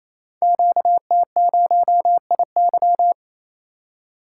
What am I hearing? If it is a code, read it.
Morse code, heard as a keyed tone, about 27 wpm, 700 Hz. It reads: QT0IY
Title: Code Groups - 5 Letter:Numbers - 28wpm